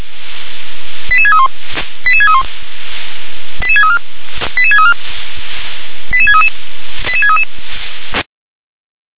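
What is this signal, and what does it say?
Five-tone selective call (ZVEI), heard as quick runs of five tones, tones of about 70 ms. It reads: 88631 88631 88634 88634 88633 88633
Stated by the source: ZVEI five-tone call 88747 received by SDR.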